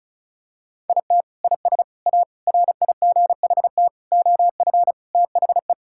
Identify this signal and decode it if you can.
Morse code, heard as a keyed tone, about 36 wpm, 700 Hz. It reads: IT IS A RIGHT OF THE
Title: Sentences from Top 1000 Words - Review Part 5 - 35wpm